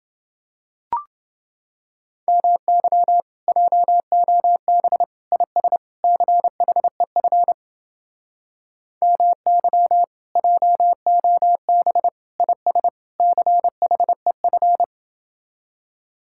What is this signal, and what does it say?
Morse code, heard as a keyed tone, about 30 wpm, 700 Hz. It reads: MY JOB IS CHEF MY JOB IS CHEF